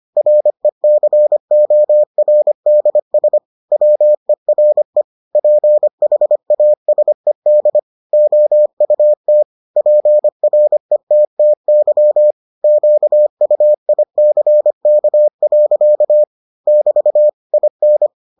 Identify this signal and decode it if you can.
Morse code, heard as a keyed tone, about 25 wpm, 600 Hz. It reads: RECORDS WERE PHASED OUT PRETTY QUICK. <BT> IN